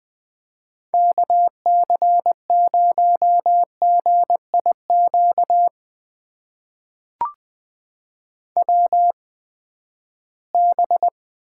Morse code, heard as a keyed tone, about 20 wpm, spaced as for 14 wpm, 700 Hz. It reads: KC0GIQ WB
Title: USA Call Signs - 6 Characters - 8x Character Spacing - 20wpm